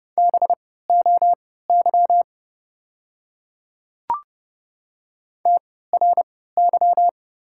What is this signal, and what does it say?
Morse code, heard as a keyed tone, about 30 wpm, spaced as for 17 wpm, 700 Hz. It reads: BOY TRY
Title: 3x Character Spacing - Top 300 Words - 30wpm